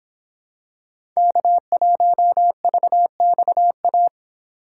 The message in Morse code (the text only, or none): K1VXA